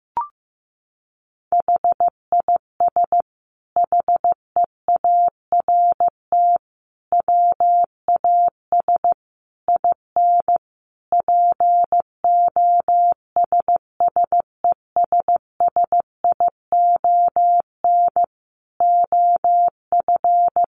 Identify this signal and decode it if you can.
Morse code, heard as a keyed tone, about 15 wpm, 700 Hz. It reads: HIS HEART WAS IN POSSESSION OF
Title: Sentences from Top 1400 Words 15wpm